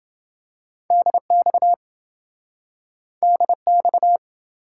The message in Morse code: DX DX

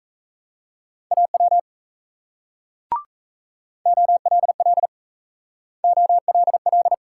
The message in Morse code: AW OLL OLL

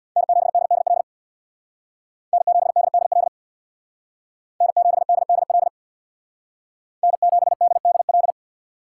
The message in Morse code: N7DDL N7DDL N7DDL N7DDL

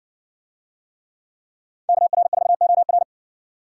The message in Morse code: KR4OR

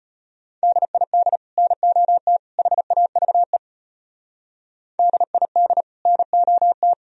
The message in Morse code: DID NOT HAVE DID NOT

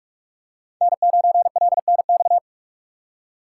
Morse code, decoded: N9LNX

